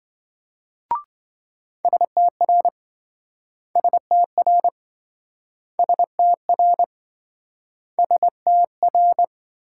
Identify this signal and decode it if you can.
Morse code, heard as a keyed tone, about 27 wpm, 700 Hz. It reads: STR STR STR STR